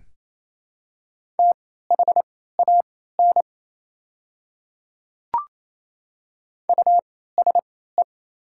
Morse code, heard as a keyed tone, about 27 wpm, spaced as for 16 wpm, 700 Hz. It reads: THAN USE
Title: 3x Character Spacing - Top 300 Words - 28wpm